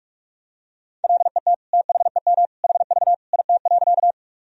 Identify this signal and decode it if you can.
Morse code, heard as a keyed tone, about 46 wpm, 700 Hz. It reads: LET THEM HV IT.
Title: Sentences from Top 300 Words and Abbreviations 45wpm